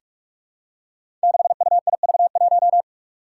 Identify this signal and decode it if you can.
Morse code, heard as a keyed tone, about 44 wpm, 700 Hz. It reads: 6UIV1
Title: Code Groups - 5 Letter:Numbers - 45wpm